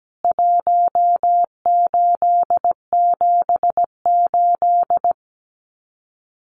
Morse code, heard as a keyed tone, about 17 wpm, 700 Hz. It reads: 1878
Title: CWT Exchange 17wpm